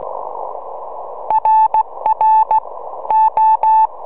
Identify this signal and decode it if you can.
Morse code, heard as a keyed tone, about 16 wpm, 890 Hz. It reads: RR O
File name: 4m